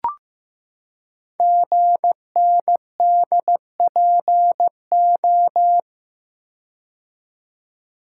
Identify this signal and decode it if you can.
Morse code, heard as a keyed tone, about 15 wpm, 700 Hz. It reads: GNDPO
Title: Code Groups - 5 Letters - 15wpm